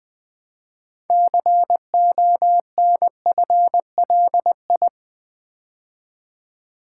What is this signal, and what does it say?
Morse code, heard as a keyed tone, about 20 wpm, 700 Hz. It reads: CONFLI
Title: Build A Word - 2 Letters At A Time - 8-Letter Words - 20wpm